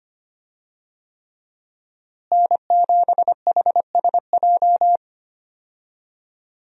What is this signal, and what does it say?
Morse code, heard as a keyed tone, about 25 wpm, 700 Hz. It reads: N7HSJ
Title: USA Call Signs - 5 Characters - Repeated 3x - 25wpm